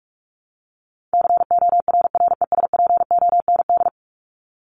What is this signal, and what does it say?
Morse code, heard as a keyed tone, about 44 wpm, 700 Hz. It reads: CORRESPOND
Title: ING Suffix - Top 100 Words with Base Words 45wpm